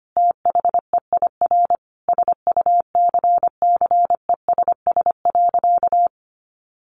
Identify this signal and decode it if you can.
Morse code, heard as a keyed tone, about 25 wpm, 700 Hz. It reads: THEIR SUCCESS.
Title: Sentences from Top 900 Words 25wpm